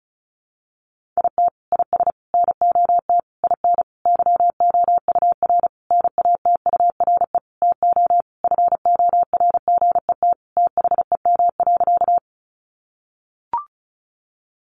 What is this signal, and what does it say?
Morse code, heard as a keyed tone, about 34 wpm, 700 Hz. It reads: IT IS NOT IN YOUR NATURE TO FORGET THEM.
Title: Sentences from Top 1200 Words 35wpm